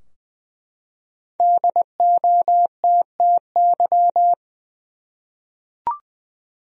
Morse code, heard as a keyed tone, about 20 wpm, 700 Hz. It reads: DOTTY
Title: USA Common Names - 20wpm